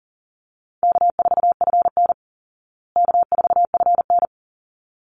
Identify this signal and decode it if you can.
Morse code, heard as a keyed tone, about 38 wpm, 700 Hz. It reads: K4FN K4FN
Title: Easy Call Signs Repeated 2x - 40wpm